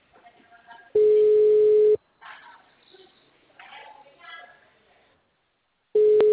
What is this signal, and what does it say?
Ringback tone (Europe, 425 Hz), with speech.